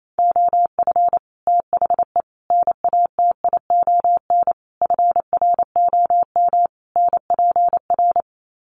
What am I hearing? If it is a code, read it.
Morse code, heard as a keyed tone, about 28 wpm, 700 Hz. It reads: OF THE NATION FROM NPR
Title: Sets of 5 Words 28wpm